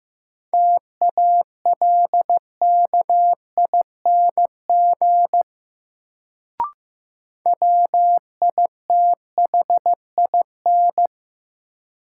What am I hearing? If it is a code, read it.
Morse code, heard as a keyed tone, about 15 wpm, 700 Hz. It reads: TALKING WITHIN
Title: Sets of 2 Words - New Word Drill - 15wpm